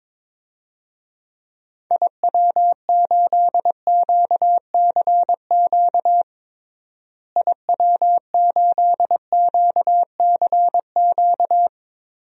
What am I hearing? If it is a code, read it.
Morse code, heard as a keyed tone, about 22 wpm, 700 Hz. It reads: IW8QCQ IW8QCQ
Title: DXCC-100 Call Signs - 6 Characters - Repeated 3x - 22wpm